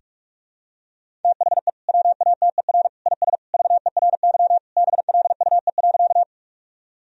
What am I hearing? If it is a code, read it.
Morse code, heard as a keyed tone, about 46 wpm, 700 Hz. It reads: THE WATER IS VERY BLUE.
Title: Sentences from Top 1000 Words - Review Part 3 - 45wpm